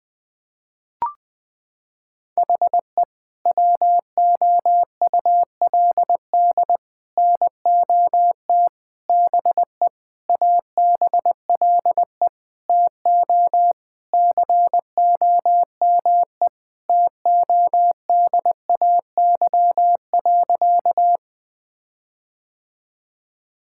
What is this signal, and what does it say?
Morse code, heard as a keyed tone, about 19 wpm, 700 Hz. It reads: HE WOULD NOT BE ABLE TO COME TODAY.
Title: Sentences with 5-Word Phrases - 20wpm